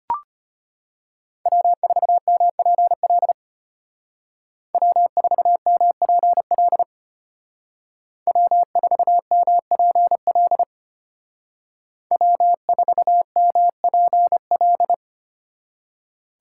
Morse code, heard as a keyed tone, about 30 wpm, 700 Hz. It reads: W4MPL W4MPL W4MPL W4MPL
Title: Speed Racing - USA 1x3 Call Signs 25wpm